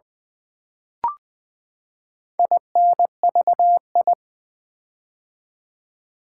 Morse code, heard as a keyed tone, about 20 wpm, 700 Hz. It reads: INVI
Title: Build A Word - 4 Letters At A Time - 10-Letter Words - 20wpm